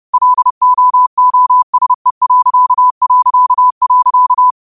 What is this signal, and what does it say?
Morse code, heard as a keyed tone, about 30 wpm, 1000 Hz. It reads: LOOSE...
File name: qotdfiles / 30wpm